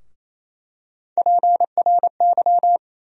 Morse code, heard as a keyed tone, about 28 wpm, 700 Hz. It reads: PRY
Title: USA Call Sign Postfixes - 3 Letters 28wpm